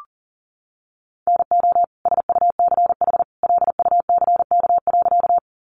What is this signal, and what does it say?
Morse code, heard as a keyed tone, about 40 wpm, 700 Hz. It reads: NO SUCH LUCK.
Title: Sentences from Top 1500 Words 40wpm